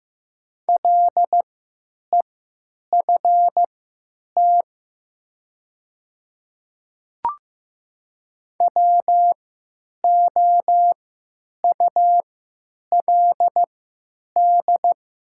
Morse code, heard as a keyed tone, about 15 wpm, spaced as for 9 wpm, 700 Hz. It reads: LEFT WOULD
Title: 3x Character Spacing - Top 300 Words - 15wpm